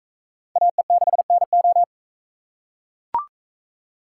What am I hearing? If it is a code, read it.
Morse code, heard as a keyed tone, about 44 wpm, 700 Hz. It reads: AE6NO